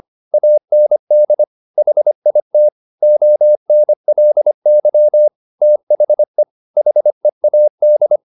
Morse code, heard as a keyed tone, about 25 wpm, 600 Hz. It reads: AND HIT ONLY THE HEAD